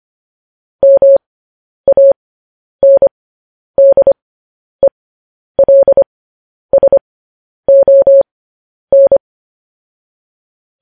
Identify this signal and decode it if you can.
Morse code, heard as a keyed tone, about 25 wpm, spaced as for 10 wpm, 550 Hz. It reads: MANDELSON